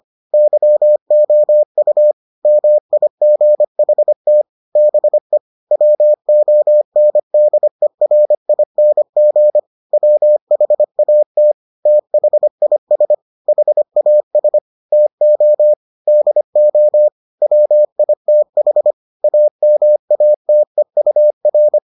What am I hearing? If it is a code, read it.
Morse code, heard as a keyed tone, about 25 wpm, 600 Hz. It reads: YOU MIGHT BE WONDERING WHAT THIS HAS TO DO WITH AMATEUR